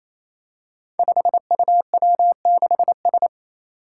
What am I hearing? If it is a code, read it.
Morse code, heard as a keyed tone, about 27 wpm, 700 Hz. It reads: 5UW6S
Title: Five Challenging Letter-Number Combinations 28wpm